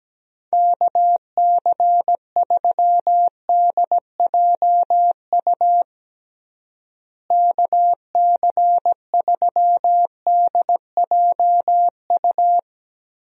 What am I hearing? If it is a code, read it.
Morse code, heard as a keyed tone, about 17 wpm, 700 Hz. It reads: KC3DJU KC3DJU
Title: USA Call Signs - 6 Characters - Repeated 3x - 17wpm